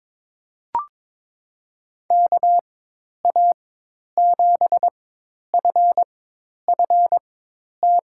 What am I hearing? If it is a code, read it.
Morse code, heard as a keyed tone, about 22 wpm, spaced as for 10 wpm, 700 Hz. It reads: KA7FFT
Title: USA Call Signs - 6 Characters - 4x Character Spacing - 22wpm